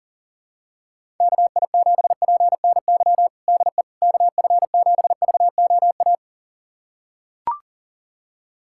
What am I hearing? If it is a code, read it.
Morse code, heard as a keyed tone, about 40 wpm, 700 Hz. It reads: KI7PNY DE KF7VOA
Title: USA Callsign DE Callsign - 6 Characters - 40wpm